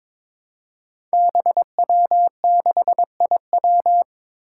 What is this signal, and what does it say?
Morse code, heard as a keyed tone, about 23 wpm, 700 Hz. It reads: BW6IW
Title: Five Challenging Letter-Number Combinations 22wpm